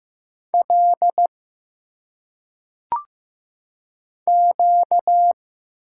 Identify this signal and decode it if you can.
Morse code, heard as a keyed tone, about 15 wpm, 700 Hz. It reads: L Q
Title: Single Letters - Phonetic Pronunciation 15wpm